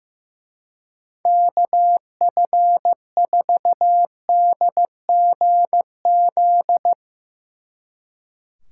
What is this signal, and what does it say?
Morse code, heard as a keyed tone, about 15 wpm, 700 Hz. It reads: KF4DGZ
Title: Speed Racing - USA 2x3 Call Signs 15wpm